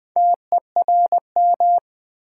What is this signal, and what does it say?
Morse code, heard as a keyed tone, about 20 wpm, 700 Hz. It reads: TERM
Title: Sentences from Top 900 Words - New Word Drill - 20wpm